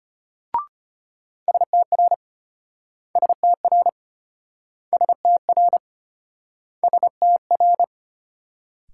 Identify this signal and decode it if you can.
Morse code, heard as a keyed tone, about 35 wpm, 700 Hz. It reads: STR STR STR STR